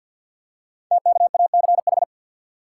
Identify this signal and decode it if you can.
Morse code, heard as a keyed tone, about 50 wpm, 690 Hz. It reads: TKACH
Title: Code Groups - 5 Letters - 50wpm